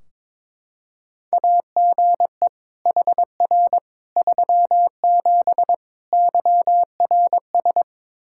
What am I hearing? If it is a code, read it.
Morse code, heard as a keyed tone, about 22 wpm, 700 Hz. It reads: AGE HR 37 YRS
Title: Practice QSO - Exchange 3 - More Info - 22wpm